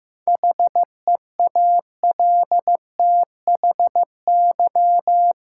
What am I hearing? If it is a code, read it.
Morse code, heard as a keyed tone, about 15 wpm, 700 Hz. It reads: HEALTHY
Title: Sentences from Top 1400 Words - New Word Drill Alt - 15wpm